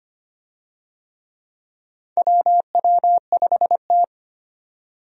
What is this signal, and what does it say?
Morse code, heard as a keyed tone, about 25 wpm, 700 Hz. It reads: WW5T
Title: USA Call Signs - 4 Characters - 25wpm